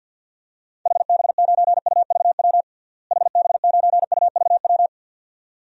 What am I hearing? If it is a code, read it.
Morse code, heard as a keyed tone, about 47 wpm, 700 Hz. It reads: HB9UVW HB9UVW
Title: DXCC-100 Call Signs - 6 Characters - Repeated 2x - 50wpm